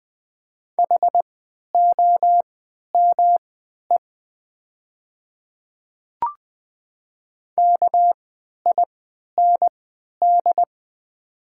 Morse code, heard as a keyed tone, about 20 wpm, spaced as for 11 wpm, 700 Hz. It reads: HOME KIND
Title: 3x Character Spacing - Top 200 Words - 20wpm